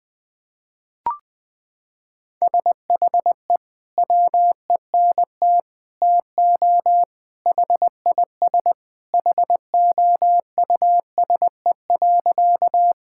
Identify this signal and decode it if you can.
Morse code, heard as a keyed tone, about 20 wpm, 700 Hz. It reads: SHE WENT TO HIS HOUSE.
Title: Sentences with 2-Word Phrases - 20wpm